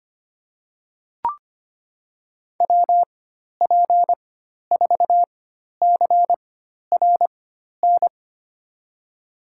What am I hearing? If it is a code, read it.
Morse code, heard as a keyed tone, about 25 wpm, spaced as for 12 wpm, 700 Hz. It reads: WP4CRN
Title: USA Call Signs - 6 Characters - 4x Character Spacing - 25wpm